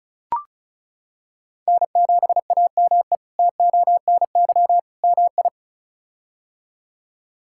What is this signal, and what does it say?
Morse code, heard as a keyed tone, about 34 wpm, 700 Hz. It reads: N7AME TONY MI
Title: K1USN SST QSO Practice - 35wpm